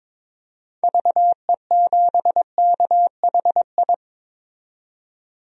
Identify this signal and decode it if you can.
Morse code, heard as a keyed tone, about 22 wpm, 700 Hz. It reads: VE7KHI
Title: DXCC-100 Call Signs - 6 Characters - 22wpm